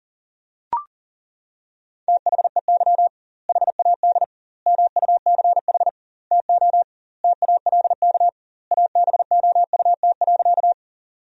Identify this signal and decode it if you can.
Morse code, heard as a keyed tone, about 40 wpm, 700 Hz. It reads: THEY HAD MUCH TO TALK ABOUT.
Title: 2x Word-Spacing - Sentences from Top 400 Words - 40wpm